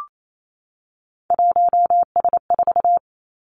Morse code, heard as a keyed tone, about 28 wpm, 700 Hz. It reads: 1S4
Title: Three Challenging Letter-Number Combinations 28wpm